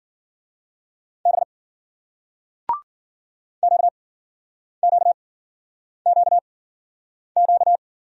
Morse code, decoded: B QQQQ

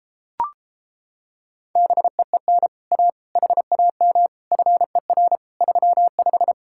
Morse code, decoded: BEEN A HAM FER 35

